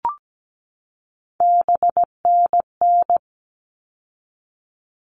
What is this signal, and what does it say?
Morse code, heard as a keyed tone, about 17 wpm, 700 Hz. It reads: BNN
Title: All Cut Numbers - 3-Digit 17wpm